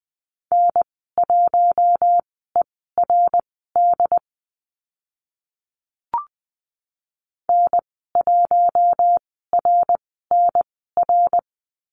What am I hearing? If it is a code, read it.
Morse code, heard as a keyed tone, about 20 wpm, spaced as for 14 wpm, 700 Hz. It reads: N1ERD N1RNR
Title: 069 Lesson 12g - Letter d - Call Signs 20wpm